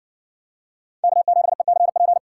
Morse code, heard as a keyed tone, about 57 wpm, 700 Hz. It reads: K7ECP